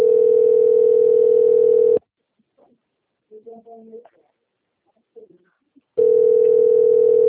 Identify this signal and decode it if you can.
Ringback tone (North America), 440 + 480 Hz.